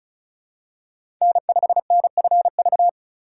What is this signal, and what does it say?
Morse code, heard as a keyed tone, about 36 wpm, 700 Hz. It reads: N5NFV
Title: USA Call Signs - 5 Characters - 4x Character Spacing - 35wpm